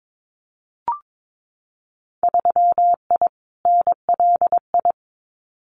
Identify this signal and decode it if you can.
Morse code, heard as a keyed tone, about 22 wpm, 700 Hz. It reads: 3I NLI
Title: Winter Field Day Exchanges Encore 22wpm